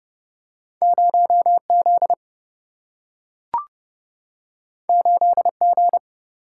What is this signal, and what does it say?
Morse code, heard as a keyed tone, about 30 wpm, 700 Hz. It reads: 0Z 8G
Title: Single-Letter Single-Number 30wpm